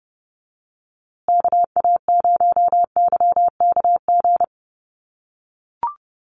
Morse code, heard as a keyed tone, about 30 wpm, 700 Hz. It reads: KA0YKG
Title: USA Call Signs Repeated 3x 30wpm